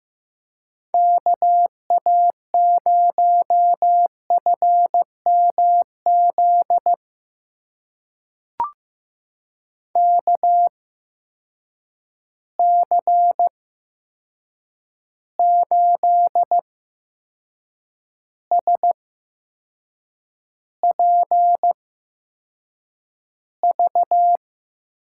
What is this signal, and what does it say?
Morse code, heard as a keyed tone, about 15 wpm, spaced as for 6 wpm, 700 Hz. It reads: KA0FMZ KC8SPV
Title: USA Call Signs - 6 Characters - 8x Character Spacing - 15wpm